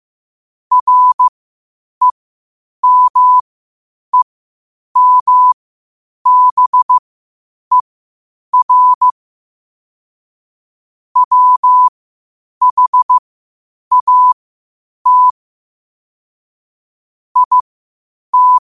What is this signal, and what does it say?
Morse code, heard as a keyed tone, about 15 wpm, spaced as for 8 wpm, 1000 Hz. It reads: REMEMBER WHAT IT
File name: qotdfiles / 8wpm